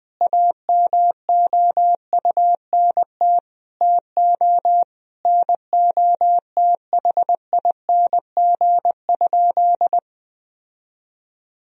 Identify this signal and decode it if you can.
Morse code, heard as a keyed tone, about 20 wpm, 700 Hz. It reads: AMOUNT TO NOTHING?
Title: Sentences from Top 1000 Words - Review Part 4 - 20wpm